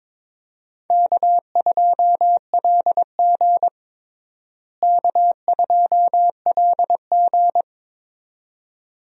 Morse code, heard as a keyed tone, about 22 wpm, 700 Hz. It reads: K2LG K2LG
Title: USA Call Signs - 4 Characters - Repeated 2x - 22wpm